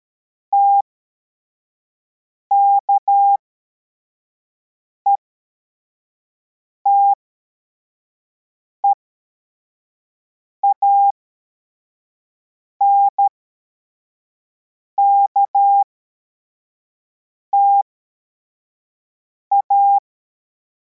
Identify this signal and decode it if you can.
Morse code, heard as a keyed tone, about 13 wpm, spaced as for 4 wpm, 800 Hz. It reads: TKETEANKTA